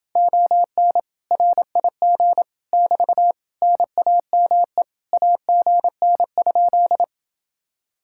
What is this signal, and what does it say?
Morse code, heard as a keyed tone, about 28 wpm, 700 Hz. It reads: ON RIG <BT> NAME AGN?